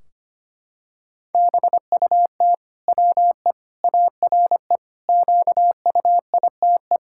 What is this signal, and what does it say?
Morse code, heard as a keyed tone, about 25 wpm, 700 Hz. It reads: BUT WE ARE QUITE
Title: Sentences from Top 1600 Words 25wpm